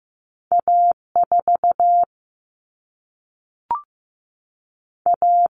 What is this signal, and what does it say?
Morse code, heard as a keyed tone, about 15 wpm, 700 Hz. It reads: A 4 A
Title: Common Cut Numbers - 2-Digit 15wpm